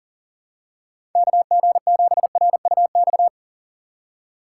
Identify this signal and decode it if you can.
Morse code, heard as a keyed tone, about 40 wpm, 700 Hz. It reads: KG7RUX
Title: USA Call Signs - 6 Characters - 3x Character Spacing - 40wpm